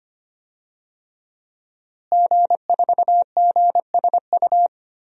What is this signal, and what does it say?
Morse code, heard as a keyed tone, about 25 wpm, 700 Hz. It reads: G4GSU
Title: DXCC-100 Call Signs - 5 Characters - Repeated 3x - 25wpm